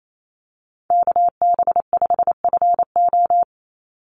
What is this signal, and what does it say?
Morse code, heard as a keyed tone, about 28 wpm, 700 Hz. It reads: KB5FO
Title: USA Call Signs 28wpm